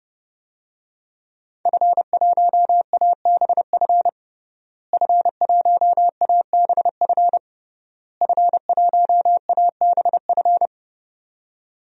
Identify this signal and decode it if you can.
Morse code, heard as a keyed tone, about 30 wpm, 700 Hz. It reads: F1ABF F1ABF F1ABF
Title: DXCC-100 Call Signs - 5 Characters - Repeated 3x - 30wpm